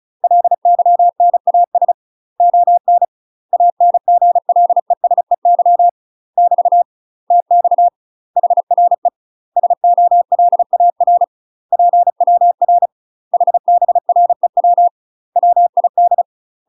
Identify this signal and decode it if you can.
Morse code, heard as a keyed tone, about 36 wpm, 700 Hz. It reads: LYNAS ON ANGLESEY <BT> TX HRE SOLAR PWR HBREW WID
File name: UK MASTER QSOs TRAINING / 35WPM / UK MASTER QSOs 0016